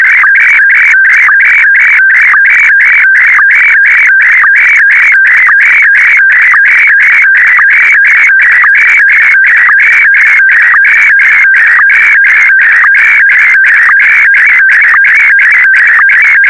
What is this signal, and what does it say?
SSTV (slow-scan television), Scottie DX.